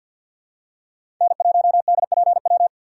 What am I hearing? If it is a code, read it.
Morse code, heard as a keyed tone, about 51 wpm, 700 Hz. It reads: N1DPW
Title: USA Call Signs - 4x Character Spacing - 50wpm